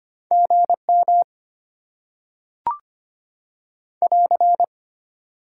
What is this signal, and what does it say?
Morse code, heard as a keyed tone, about 25 wpm, 700 Hz. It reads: GM <AR>
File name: Top 90 QSO Elements - 6x Character Spacing - 25wpm